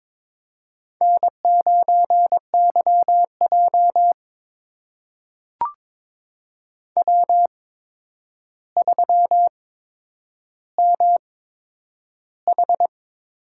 Morse code, heard as a keyed tone, about 22 wpm, spaced as for 9 wpm, 700 Hz. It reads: N9YJ W3MH
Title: USA Call Signs - 4 Characters - 8x Character Spacing - 22wpm